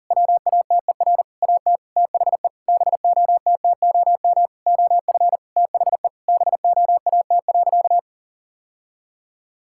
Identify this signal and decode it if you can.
Morse code, heard as a keyed tone, about 40 wpm, 700 Hz. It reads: WATER AT THE BOTTOM OF THE BOAT.
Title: Sentences from Top 1000 Words - Review Part 6 - 40wpm